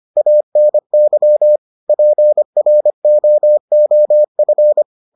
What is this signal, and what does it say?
Morse code, heard as a keyed tone, about 25 wpm, 600 Hz. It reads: ANY PROOF